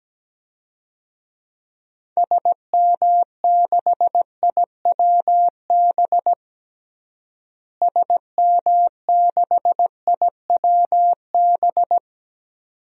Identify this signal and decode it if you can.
Morse code, heard as a keyed tone, about 17 wpm, 700 Hz. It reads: SM6IWB SM6IWB
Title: DXCC-100 Call Signs - 6 Characters - Repeated 2x - 17wpm